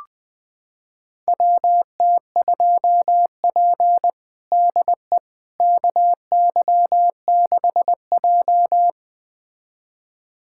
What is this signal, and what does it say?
Morse code, heard as a keyed tone, about 20 wpm, 700 Hz. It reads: WT2P DE KY6J
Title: USA Callsign DE Callsign - 4 Characters - 20wpm